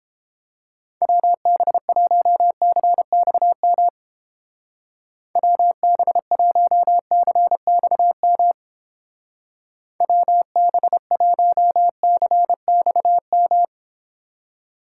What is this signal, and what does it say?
Morse code, heard as a keyed tone, about 31 wpm, 700 Hz. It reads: WB1CXM WB1CXM WB1CXM